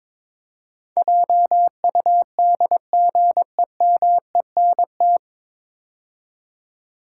Morse code, heard as a keyed tone, about 21 wpm, 700 Hz. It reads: JUDGEMENT
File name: 9 Letter Words - First 3 Letters Given - 22wpm